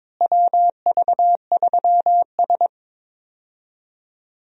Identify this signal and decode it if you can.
Morse code, heard as a keyed tone, about 22 wpm, 700 Hz. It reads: WV3S
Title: USA Call Signs - 4 Characters - Repeated 3x - 22wpm